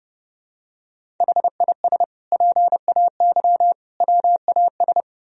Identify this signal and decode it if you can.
Morse code, heard as a keyed tone, about 30 wpm, 700 Hz. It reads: HIS PAY WAS
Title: Sentences from Top 1000 Words - Review Part 1 - 30wpm